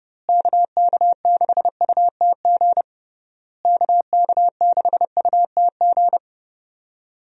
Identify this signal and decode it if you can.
Morse code, heard as a keyed tone, about 28 wpm, 700 Hz. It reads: KK6UTG KK6UTG
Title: USA Call Signs - 6 Characters - Repeated 3x - 30wpm